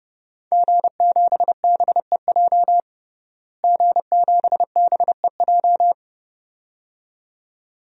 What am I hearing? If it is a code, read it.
Morse code, heard as a keyed tone, about 30 wpm, 700 Hz. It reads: G7BEJ G7BEJ
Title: DXCC-100 Call Signs - 5 Characters - Repeated 2x - 30wpm